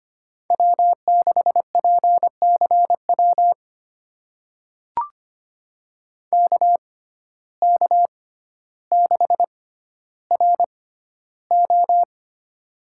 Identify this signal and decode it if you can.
Morse code, heard as a keyed tone, about 25 wpm, spaced as for 13 wpm, 700 Hz. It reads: W6PCW KK6RO
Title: USA Call Signs - 5 Characters - 6x Character Spacing - 25wpm